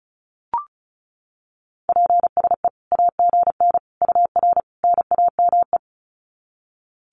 Morse code, heard as a keyed tone, about 34 wpm, 700 Hz. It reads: PSE AGN UR NAME
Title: Common QSO Phrases 35wpm